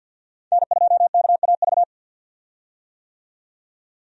Morse code, heard as a keyed tone, about 51 wpm, 700 Hz. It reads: N2KAV
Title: USA Call Signs - 5 Characters - 50wpm